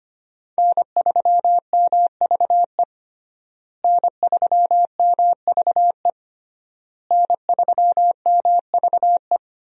Morse code, heard as a keyed tone, about 25 wpm, 700 Hz. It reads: N3MVE N3MVE N3MVE
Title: USA Call Signs - 5 Characters - Repeated 3x - 25wpm